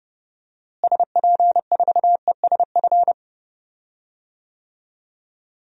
Morse code, heard as a keyed tone, about 32 wpm, 700 Hz. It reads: SP4ESF
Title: DXCC-100 Call Signs - 6 Characters - Repeated 2x - 30wpm